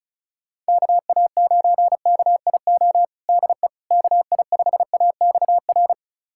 Morse code, heard as a keyed tone, about 36 wpm, 700 Hz. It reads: KA9KIO DE KI5AXR
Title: USA Callsign DE Callsign - 6 Characters - 35wpm